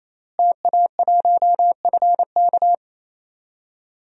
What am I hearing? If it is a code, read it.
Morse code, heard as a keyed tone, about 28 wpm, 700 Hz. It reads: TA1FK
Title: DXCC-100 Call Signs - 5 Characters - Repeated 3x - 28wpm